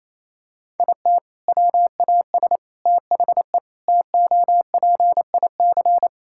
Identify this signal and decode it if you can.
Morse code, heard as a keyed tone, about 28 wpm, 700 Hz. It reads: IT WAS THE TOPIC
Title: Sentences from Top 1600 Words 28wpm